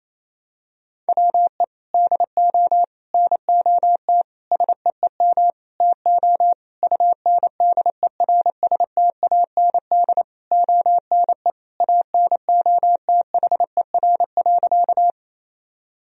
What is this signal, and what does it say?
Morse code, heard as a keyed tone, about 28 wpm, 700 Hz. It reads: WE DO NOT SEEM TO UNDERSTAND ONE ANOTHER.
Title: Sentences from Top 1000 Words - Review Part 1 - 28wpm